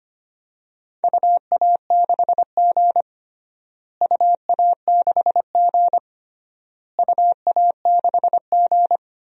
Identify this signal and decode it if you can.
Morse code, heard as a keyed tone, about 25 wpm, 700 Hz. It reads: UA6G UA6G UA6G
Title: DXCC-100 Call Signs - 4 Characters - Repeated 3x - 25wpm